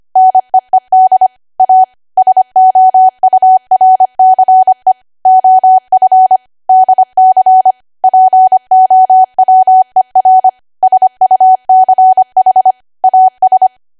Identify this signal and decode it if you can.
Morse code, heard as a keyed tone, about 25 wpm, 750 Hz. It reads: NEED A SOURCE OF DC POWER SUCH AS
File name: cwa / W1AW-Level3 / 25WPM